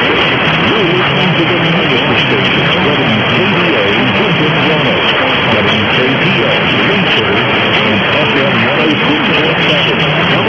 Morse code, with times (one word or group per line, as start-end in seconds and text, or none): none